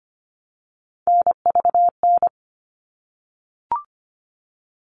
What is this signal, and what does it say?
Morse code, heard as a keyed tone, about 25 wpm, 700 Hz. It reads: NVN
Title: All Cut Numbers - 3-Digit 25wpm